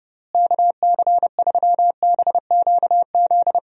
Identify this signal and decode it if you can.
Morse code, heard as a keyed tone, about 30 wpm, 700 Hz. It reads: KC3BQZ
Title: USA Call Signs - 6 Characters - Repeated 2x - 30wpm